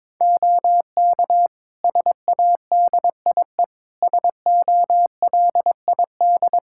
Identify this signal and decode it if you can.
Morse code, heard as a keyed tone, about 22 wpm, 700 Hz. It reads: OK SADIE SOLID
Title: Practice QSO - Exchange 3 - More Info - 22wpm